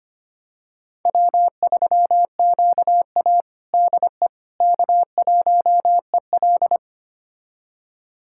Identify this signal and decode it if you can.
Morse code, heard as a keyed tone, about 25 wpm, 700 Hz. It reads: W3QA DE K1EL
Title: USA Callsign DE Callsign - 4 Characters - 25wpm